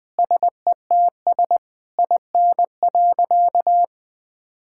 Morse code, heard as a keyed tone, about 20 wpm, 700 Hz. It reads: SETS IN.
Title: Difficult Sentences with Hints - 20wpm